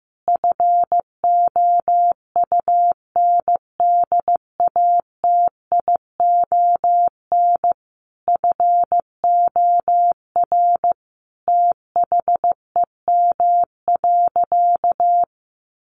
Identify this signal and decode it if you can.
Morse code, heard as a keyed tone, about 15 wpm, 700 Hz. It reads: FOUNDATION FOR THEM.
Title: Sentences from Top 1200 Words 15wpm